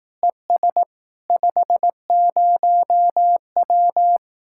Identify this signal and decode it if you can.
Morse code, heard as a keyed tone, about 18 wpm, 700 Hz. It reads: ES 50W
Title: Speed-Racing Common QSO Phrases Encore - 15wpm